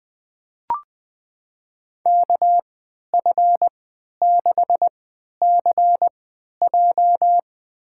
Morse code, heard as a keyed tone, about 21 wpm, spaced as for 11 wpm, 700 Hz. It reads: KF6CJ
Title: USA Call Signs - 3x Character Spacing - 20wpm